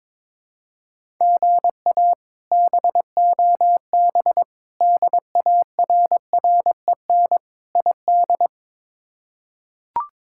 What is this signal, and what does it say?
Morse code, heard as a keyed tone, about 22 wpm, 700 Hz. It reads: GA BOB DARREN ID
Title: K1USN SST QSO Practice - 22wpm